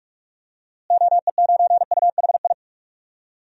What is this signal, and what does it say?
Morse code, heard as a keyed tone, about 46 wpm, 700 Hz. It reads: OE9UHI
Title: DXCC-100 Call Signs - 6 Characters - Repeated 2x - 45wpm